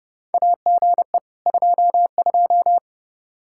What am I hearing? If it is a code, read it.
Morse code, heard as a keyed tone, about 28 wpm, 700 Hz. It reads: AGE 22
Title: Practice QSO - Exchange 3 - More Info - 30wpm